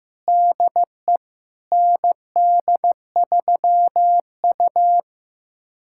Morse code, heard as a keyed tone, about 16 wpm, 700 Hz. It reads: DE ND3U
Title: USA Callsign DE Callsign - 4 Characters - 15wpm